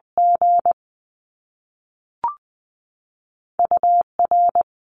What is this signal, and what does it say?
Morse code, heard as a keyed tone, about 20 wpm, 700 Hz. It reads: G UR
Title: Top Two Letter Combinations in Words 20wpm